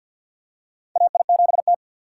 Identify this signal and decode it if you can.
Morse code, heard as a keyed tone, about 51 wpm, 700 Hz. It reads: AI7T
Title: USA Call Signs - 3x Character Spacing - 50wpm